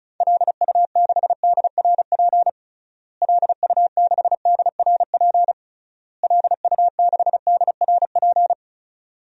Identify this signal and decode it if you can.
Morse code, heard as a keyed tone, about 36 wpm, 700 Hz. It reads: LU6DRP LU6DRP LU6DRP
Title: DXCC-100 Call Signs - 6 Characters - Repeated 3x - 35wpm